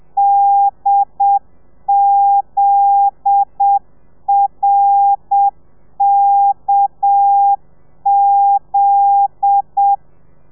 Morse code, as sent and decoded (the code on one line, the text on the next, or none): -.. --.. .-. -.- --..
DZRKZ